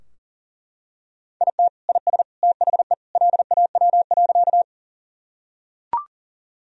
Morse code, heard as a keyed tone, about 40 wpm, 700 Hz. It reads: IT IS THE LAW.
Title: Sentences from Top 1000 Words - Review Part 3 - 40wpm